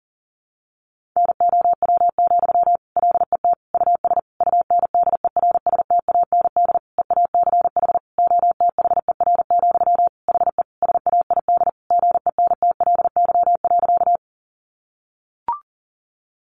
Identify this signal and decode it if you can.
Morse code, heard as a keyed tone, about 37 wpm, 700 Hz. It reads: NOW, LET US UNDERSTAND EACH OTHER, HE SAID GENTLY.
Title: Sentences from Top 1500 Words 40wpm